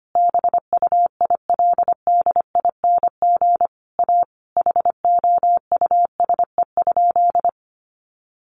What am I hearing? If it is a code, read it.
Morse code, heard as a keyed tone, about 25 wpm, 700 Hz. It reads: BUILDING A HOUSE?
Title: Sentences from Top 700 Words 25wpm